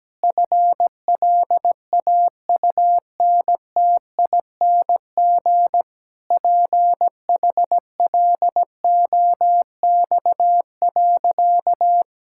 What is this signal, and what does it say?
Morse code, heard as a keyed tone, about 17 wpm, 700 Hz. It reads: FLAUNTING PHLOX.
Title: Pangram Challenge v2 - 17wpm